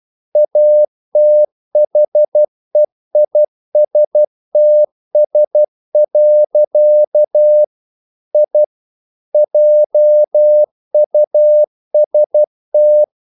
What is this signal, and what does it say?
Morse code, heard as a keyed tone, about 12 wpm, 600 Hz. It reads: ATHEISTS. I JUST
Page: English quotations in CW (telegraphy - morse)